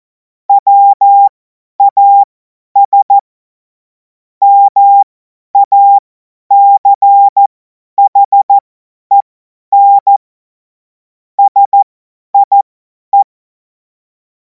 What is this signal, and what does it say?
Morse code, heard as a keyed tone, about 14 wpm, spaced as for 10 wpm, 800 Hz. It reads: WAS MACHEN SIE
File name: Morsekurs / Witze / Witz 10. Februar 2026 50 F70